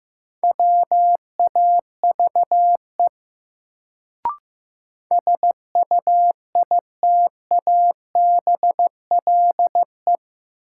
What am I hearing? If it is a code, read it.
Morse code, heard as a keyed tone, about 15 wpm, 700 Hz. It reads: WAVE SUITABLE
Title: Sentences from Top 1400 Words - New Word Drill - 15wpm